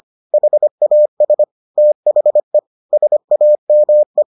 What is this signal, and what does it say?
Morse code, heard as a keyed tone, about 25 wpm, 600 Hz. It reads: HAS THE SAME